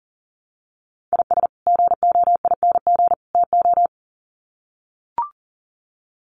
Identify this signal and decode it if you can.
Morse code, heard as a keyed tone, about 40 wpm, 700 Hz. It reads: IS GOING TO
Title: Sets of 3 Words 40wpm